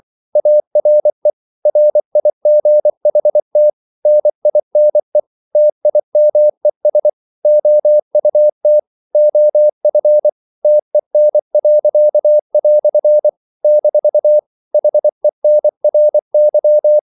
Morse code, heard as a keyed tone, about 24 wpm, 600 Hz. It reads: ARE RIGHT NINE TIMES OUT OF TEN." - HENRY
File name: CW / Practice / EnglishQuotations0324